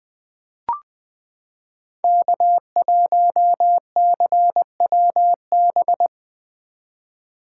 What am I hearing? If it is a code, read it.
Morse code, heard as a keyed tone, about 20 wpm, 700 Hz. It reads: K1CWB
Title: 2023 POTA Hunter Call Signs - 20wpm